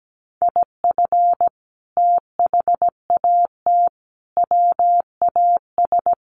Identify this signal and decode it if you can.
Morse code, heard as a keyed tone, about 17 wpm, 700 Hz. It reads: IF THAT WAS
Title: Sentences from Top 1400 Words 17wpm